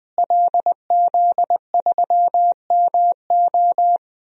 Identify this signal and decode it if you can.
Morse code, heard as a keyed tone, about 20 wpm, 700 Hz. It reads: LZ3MO
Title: DXCC-100 Call Signs - 5 Characters - Repeated 2x - 20wpm